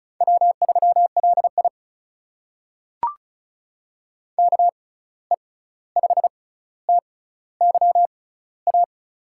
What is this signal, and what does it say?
Morse code, heard as a keyed tone, about 36 wpm, spaced as for 16 wpm, 700 Hz. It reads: W3LI KE5TYA
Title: USA Call Signs - 6x Character Spacing - 35wpm